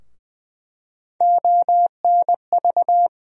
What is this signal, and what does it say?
Morse code, heard as a keyed tone, about 21 wpm, 700 Hz. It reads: ONV
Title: USA Call Sign Postfixes - 3 Letters 20wpm